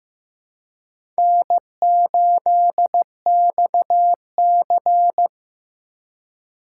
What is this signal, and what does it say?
Morse code, heard as a keyed tone, about 15 wpm, 700 Hz. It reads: N8XC
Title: USA Call Signs - 2x Character Spacing - 15wpm